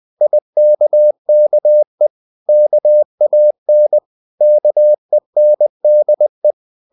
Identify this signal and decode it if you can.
Morse code, heard as a keyed tone, about 20 wpm, 600 Hz. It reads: IKKE KAN KENDE
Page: CW med Gnister